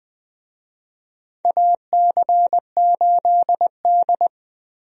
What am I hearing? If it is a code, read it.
Morse code, heard as a keyed tone, about 20 wpm, 700 Hz. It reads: AC8D
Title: USA Call Signs - 4 Characters - Repeated 3x - 20wpm